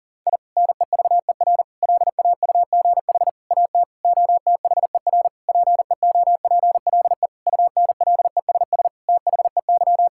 Morse code, heard as a keyed tone, about 40 wpm, 700 Hz. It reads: I NEVER LAUGH AT OTHER PEOPLE UNLESS THEY
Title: Sentences from Top 1000 Words - Review Part 6 - 40wpm